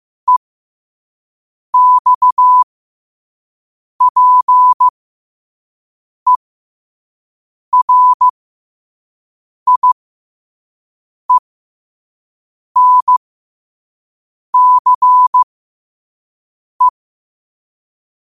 Morse code, heard as a keyed tone, about 15 wpm, spaced as for 5 wpm, 1000 Hz. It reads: EXPERIENCE
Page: New quotes every day in morse code at 5 Words per minute.